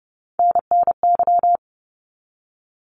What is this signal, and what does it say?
Morse code, heard as a keyed tone, about 30 wpm, 700 Hz. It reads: NNY
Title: ARRL Sections 30wpm